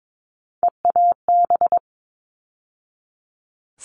Morse code, heard as a keyed tone, about 22 wpm, 700 Hz. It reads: EAB
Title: All Cut Numbers - 3-Digit 22wpm